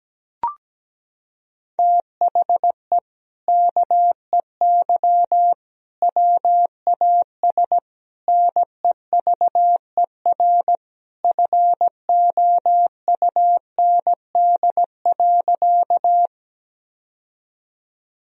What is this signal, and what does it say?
Morse code, heard as a keyed tone, about 17 wpm, 700 Hz. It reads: THE KEY WAS NEVER FOUND.
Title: Sentences from Top 1000 Words - Review Part 4 - 17wpm